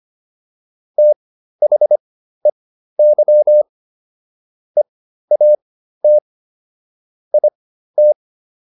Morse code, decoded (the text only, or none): THEY EAT IT